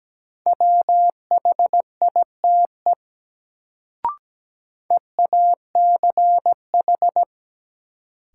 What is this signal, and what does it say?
Morse code, heard as a keyed tone, about 17 wpm, 700 Hz. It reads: WHITE EACH
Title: Sets of 2 Words - New Word Drill - 17wpm